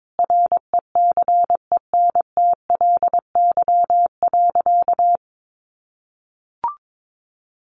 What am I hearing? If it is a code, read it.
Morse code, heard as a keyed tone, about 22 wpm, 700 Hz. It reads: RECENTLY.
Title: Sentences from Top 1200 Words 22wpm